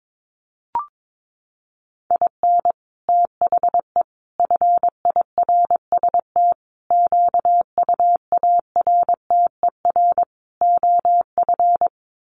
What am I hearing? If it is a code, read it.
Morse code, heard as a keyed tone, about 22 wpm, 700 Hz. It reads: IN THE FIRST QUARTER OF
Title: Sets of 5 Words Encore2 22wpm